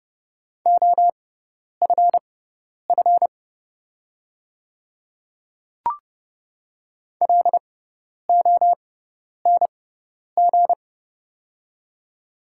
Morse code, heard as a keyed tone, about 30 wpm, spaced as for 10 wpm, 700 Hz. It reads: OFF LONG
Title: 6x Character Spacing - Top 300 Words - 30wpm